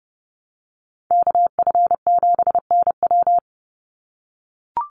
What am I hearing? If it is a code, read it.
Morse code, heard as a keyed tone, about 30 wpm, 700 Hz. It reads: KF7NW
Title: USA Call Signs 30wpm